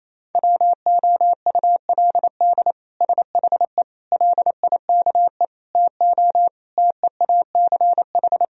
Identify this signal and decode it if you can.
Morse code, heard as a keyed tone, about 28 wpm, 700 Hz. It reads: WOULD SHE LIKE TO TEACH
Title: Sentences from Top 1000 Words - Review Part 2 - 28wpm